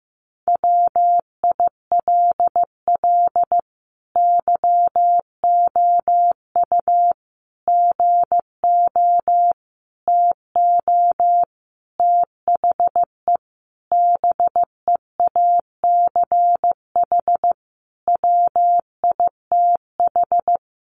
Morse code, heard as a keyed tone, about 15 wpm, 700 Hz. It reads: WILL YOU GO TO THE BEACH WITH
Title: Sentences from Top 1500 Words 15wpm